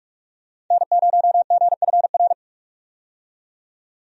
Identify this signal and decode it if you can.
Morse code, heard as a keyed tone, about 43 wpm, 700 Hz. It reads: N0GFR
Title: USA Call Signs - 5 Characters - 45wpm